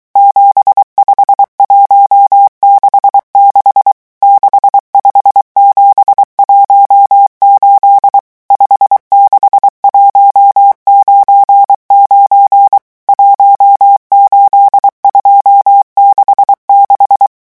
Morse code, heard as a keyed tone, about 23 wpm, 800 Hz. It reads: 75166 65718 56199 18266
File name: morse mp3 / revision11 chiffres vitesse 22